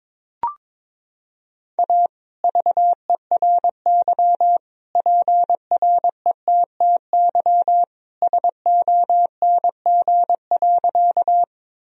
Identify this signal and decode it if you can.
Morse code, heard as a keyed tone, about 22 wpm, 700 Hz. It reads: A VERY PRETTY SONG.
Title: Sentences from Top 1000 Words - Review Part 2 - 22wpm